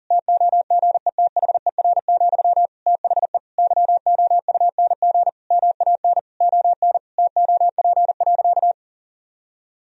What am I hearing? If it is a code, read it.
Morse code, heard as a keyed tone, about 40 wpm, 700 Hz. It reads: TOGETHER, THE YOUNG MAN ON TOP.
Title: Sentences from Top 1000 Words - Review Part 2 - 40wpm